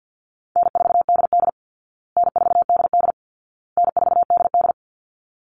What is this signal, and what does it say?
Morse code, heard as a keyed tone, about 48 wpm, 700 Hz. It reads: N4DD N4DD N4DD
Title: Easy Call Signs Repeated 3x - 50wpm